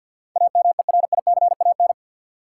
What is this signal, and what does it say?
Morse code, heard as a keyed tone, about 52 wpm, 700 Hz. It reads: AMERICAN
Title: Sets of 2 Words - New Word Drill - 50wpm